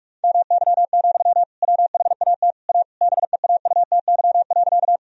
Morse code, heard as a keyed tone, about 41 wpm, 700 Hz. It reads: MY, WHAT A BEAUTY.
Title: Sentences from Top 1000 Words - Review Part 2 - 45wpm